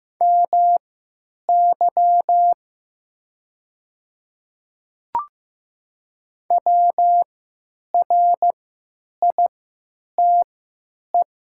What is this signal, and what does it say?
Morse code, heard as a keyed tone, about 15 wpm, spaced as for 9 wpm, 700 Hz. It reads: MY WRITE
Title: 3x Character Spacing - Top 100 Words - 15wpm